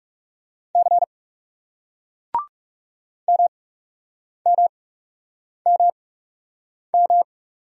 Morse code, heard as a keyed tone, about 41 wpm, spaced as for 8 wpm, 700 Hz. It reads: C MMMM